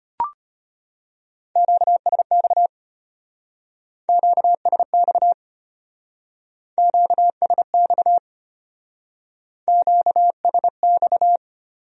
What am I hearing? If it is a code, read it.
Morse code, heard as a keyed tone, about 30 wpm, 700 Hz. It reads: QSX QSX QSX QSX
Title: Speed-Racing Q-Codes - 25wpm